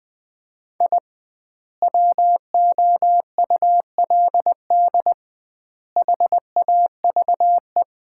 Morse code, heard as a keyed tone, about 20 wpm, 700 Hz. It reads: I WOULD HAVE
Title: 2x Word-Spacing - Sets of 3 Words - 20wpm